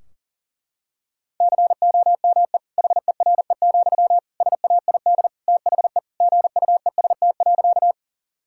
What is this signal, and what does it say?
Morse code, heard as a keyed tone, about 40 wpm, 700 Hz. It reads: COME HERE, SAID THE GUEST.
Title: Sentences from Top 1600 Words 40wpm